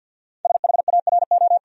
Morse code, heard as a keyed tone, about 51 wpm, 700 Hz. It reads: SHARO
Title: USA Common Names - 50wpm